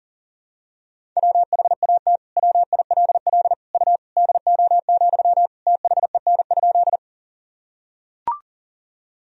Morse code, heard as a keyed tone, about 37 wpm, 700 Hz. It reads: WHAT WILL U DO, THEN?
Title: Sentences from Top 100 Words and Abbreviations 40wpm